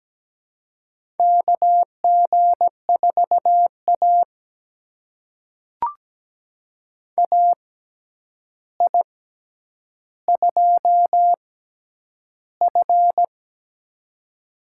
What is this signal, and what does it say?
Morse code, heard as a keyed tone, about 17 wpm, spaced as for 9 wpm, 700 Hz. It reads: KG4A AI2F
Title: USA Call Signs - 4 Characters - 6x Character Spacing - 17wpm